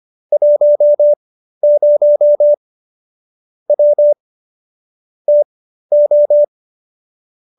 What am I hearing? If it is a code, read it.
Morse code, heard as a keyed tone, about 25 wpm, spaced as for 13 wpm, 590 Hz. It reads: 10 W TO